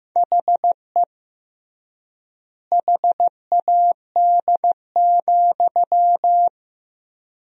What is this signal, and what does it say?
Morse code, heard as a keyed tone, about 15 wpm, 700 Hz. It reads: HE HAD,
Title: 3x Word-Spacing - Sentences from Top 500 Words - 15wpm